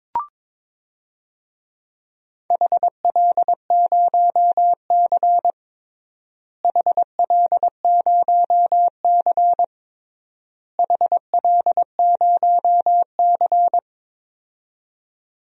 Morse code, heard as a keyed tone, about 22 wpm, 700 Hz. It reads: HL0C HL0C HL0C
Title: DXCC-100 Call Signs - 4 Characters - Repeated 3x - 22wpm